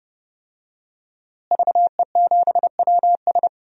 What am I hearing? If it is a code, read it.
Morse code, heard as a keyed tone, about 32 wpm, 700 Hz. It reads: VE7WS
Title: DXCC-100 Call Signs - 5 Characters - Repeated 2x - 30wpm